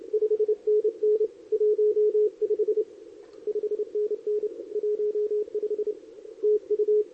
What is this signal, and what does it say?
Morse code, heard as a keyed tone, about 29 wpm, 410 Hz. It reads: 5NN 15 5NN 15 TU
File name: CQWWCW2023 / SP